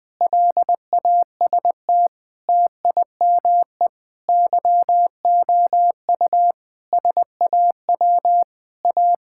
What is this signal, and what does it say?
Morse code, heard as a keyed tone, about 20 wpm, 700 Hz. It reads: LAST TIME YOU SAW A
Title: Sentences with 8-Word Phrases - 20wpm